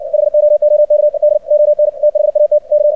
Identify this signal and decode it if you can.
Morse code, heard as a keyed tone, about 60 wpm, 600 Hz. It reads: HOOKD ON THAT K
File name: archived / CRSnet